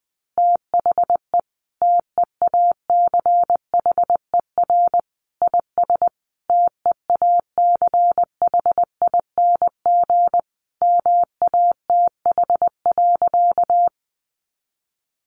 Morse code, heard as a keyed tone, about 20 wpm, 700 Hz. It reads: THE TEACHER IS TEACHING MATH.
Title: Easy Phrases 20wpm